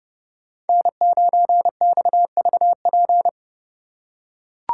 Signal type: Morse code

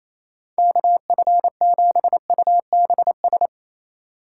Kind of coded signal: Morse code

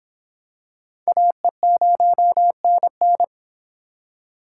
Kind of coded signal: Morse code

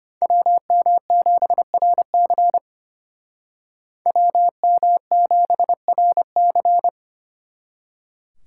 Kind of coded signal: Morse code

